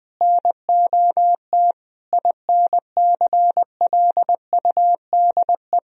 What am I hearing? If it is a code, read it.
Morse code, heard as a keyed tone, about 20 wpm, 700 Hz. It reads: NOT INCLUDE
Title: Sentences from Top 1000 Words - Review Part 3 - 20wpm